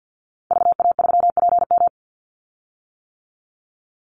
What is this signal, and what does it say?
Morse code, heard as a keyed tone, about 50 wpm, 700 Hz. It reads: VA3PM
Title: Call Signs - 50wpm